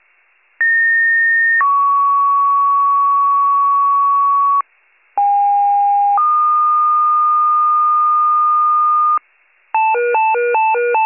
Two-tone page, heard as a one-second tone, then a three-second tone, about 1850 then 1120 Hz, about 810 then 1220 Hz. It is heard from 0.6 to 9.2 s.